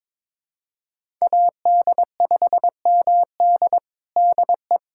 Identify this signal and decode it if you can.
Morse code, heard as a keyed tone, about 22 wpm, 700 Hz. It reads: AD5MD DE
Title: USA Callsign DE Callsign - 5 Characters - 22wpm